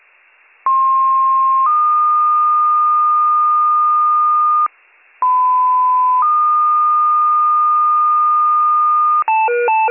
Two-tone page, heard as a one-second tone, then a three-second tone, about 1060 then 1220 Hz, about 1010 then 1220 Hz. It is heard from 0.7 to 9.2 s.